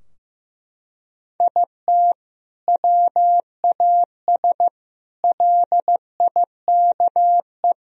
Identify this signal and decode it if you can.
Morse code, heard as a keyed tone, about 15 wpm, 700 Hz. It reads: IT WAS LIKE
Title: Sentences from Top 1000 Words - Review Part 3 - 15wpm